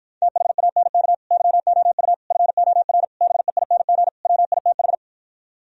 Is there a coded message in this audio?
Yes: Morse code from 0.2 to 4.9 s, heard as a keyed tone, about 46 wpm, 700 Hz. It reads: THANK YOU FOR BEING WITH